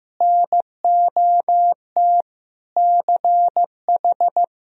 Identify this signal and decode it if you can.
Morse code, heard as a keyed tone, about 16 wpm, 700 Hz. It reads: NOT CH
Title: Sentences from Top 1000 Words - Review Part 3 - 15wpm